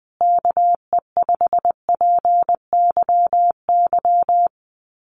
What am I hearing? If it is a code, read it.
Morse code, heard as a keyed tone, about 20 wpm, 700 Hz. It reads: KE5PYY
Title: USA Call Signs Repeated 2x 20wpm